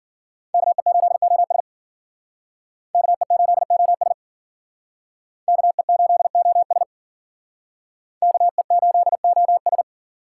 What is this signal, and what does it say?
Morse code, heard as a keyed tone, about 50 wpm, 700 Hz. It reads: KE8OS KE8OS KE8OS KE8OS
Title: Speed Racing - USA 2x2 Call Signs 40wpm